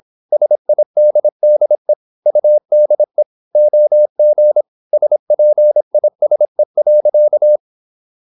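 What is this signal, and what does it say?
Morse code, heard as a keyed tone, about 26 wpm, 600 Hz. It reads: SIDDE UDE OG SPISE.
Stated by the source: High-speed (QRQ) Morse at 26 wpm.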